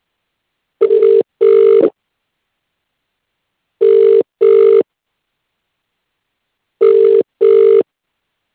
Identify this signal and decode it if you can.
Ringback tone (UK), 400 + 450 Hz.